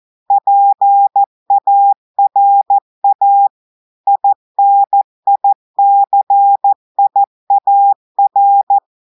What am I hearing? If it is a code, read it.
Morse code, heard as a keyed tone, about 14 wpm, 800 Hz. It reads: PARA INICIAR
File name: audios / portugais / portugais texte en clair 2 vitesse 14 mots minutes